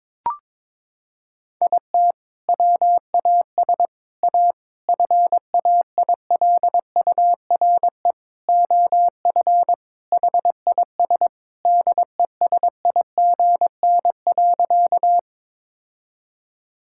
Morse code, heard as a keyed tone, about 21 wpm, 700 Hz. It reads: IT WAS A FAILURE OF HIS DESIGN.
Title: Sentences from Top 1000 Words - Review Part 6 - 22wpm